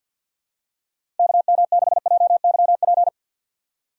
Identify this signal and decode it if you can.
Morse code, heard as a keyed tone, about 47 wpm, 700 Hz. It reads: KM6JYP
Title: USA Call Signs - 4x Character Spacing - 50wpm